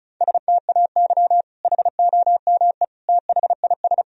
Morse code, heard as a keyed tone, about 36 wpm, 700 Hz. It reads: STAY HOME THIS
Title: Sentences from Top 1000 Words - Review Part 1 - 35wpm